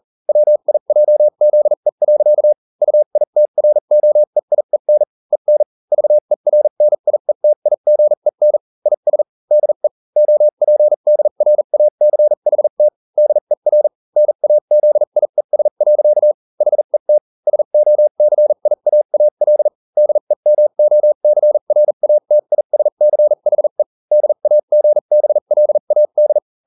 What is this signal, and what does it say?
Morse code, heard as a keyed tone, about 40 wpm, spaced as for 35 wpm, 600 Hz. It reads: WIJZE. UITROEIEN EN VERNIETIGEN IS DE OPDRACHT DER NAZIES. HET SOCIAAL DEMOCRATISCHE DAGBLAD